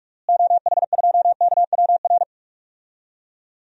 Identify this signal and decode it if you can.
Morse code, heard as a keyed tone, about 43 wpm, 700 Hz. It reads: OH2KWR
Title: DXCC-100 Call Signs - 6 Characters - Repeated 2x - 45wpm